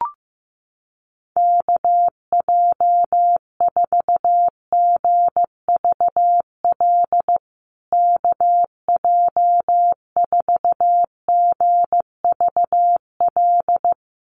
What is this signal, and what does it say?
Morse code, heard as a keyed tone, about 15 wpm, 700 Hz. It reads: KJ4GVL KJ4GVL
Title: USA Call Signs Repeated 3x 15wpm